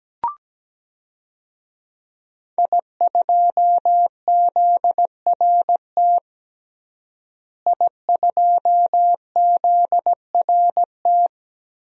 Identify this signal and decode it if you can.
Morse code, heard as a keyed tone, about 17 wpm, 700 Hz. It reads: I2ZRT I2ZRT
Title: DXCC-100 Call Signs - 5 Characters - Repeated 3x - 17wpm